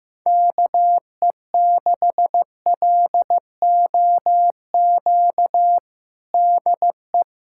Morse code, heard as a keyed tone, about 16 wpm, 700 Hz. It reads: KE6LOQ DE
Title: USA Callsign DE Callsign 15wpm